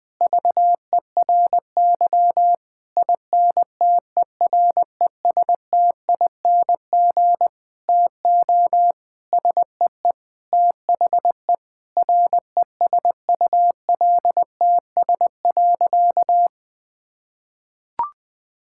Morse code, heard as a keyed tone, about 20 wpm, 700 Hz. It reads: VERY INTERESTING TO SEE THE RESULTS.
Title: Sentences with 7-Word Phrases - 20wpm